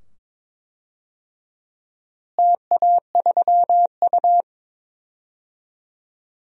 Morse code, heard as a keyed tone, about 22 wpm, 700 Hz. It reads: TA3U
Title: DXCC-100 Call Signs - 4 Characters - Repeated 2x - 22wpm